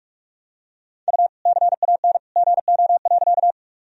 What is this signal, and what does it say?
Morse code, heard as a keyed tone, about 45 wpm, 700 Hz. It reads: U CAN GO.
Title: Sentences from Top 100 Words and Abbreviations 45wpm